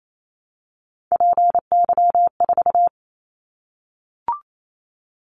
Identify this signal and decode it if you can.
Morse code, heard as a keyed tone, about 28 wpm, 700 Hz. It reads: PY4
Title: Callsign Prefixes for Top 50 DXCC - 28wpm